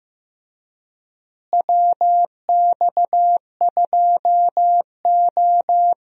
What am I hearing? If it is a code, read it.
Morse code, heard as a keyed tone, about 15 wpm, 700 Hz. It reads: WX2O
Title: USA Call Signs - 4 Characters - Repeated 2x - 15wpm